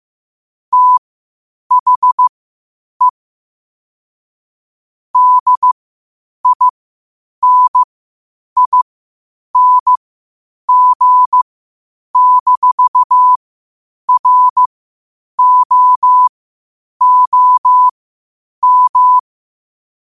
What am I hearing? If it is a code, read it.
Morse code, heard as a keyed tone, about 15 wpm, spaced as for 8 wpm, 1000 Hz. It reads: THE DINING-ROOM
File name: qotdfiles / 8wpm